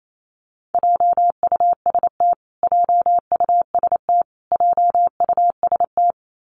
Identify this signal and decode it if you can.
Morse code, heard as a keyed tone, about 28 wpm, 700 Hz. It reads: JUST JUST JUST
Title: Top 500 Words Repeated 28wpm